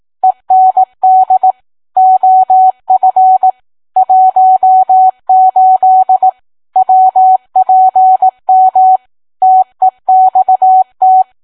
Morse code, heard as a keyed tone, about 18 wpm, 750 Hz. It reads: END OF 18 WPM TEXT